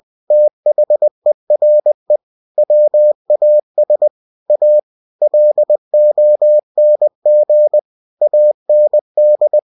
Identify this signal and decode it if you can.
Morse code, heard as a keyed tone, about 20 wpm, 600 Hz. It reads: THERE WAS A LONG AND
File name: cw-books / winnie-the-pooh / winnie-20wpm-page-0060